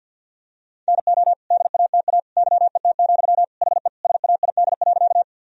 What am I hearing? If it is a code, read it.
Morse code, heard as a keyed tone, about 50 wpm, 700 Hz. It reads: NO DATA YET, HE SAID.
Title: Sentences from Top 1000 Words - Review Part 3 - 50wpm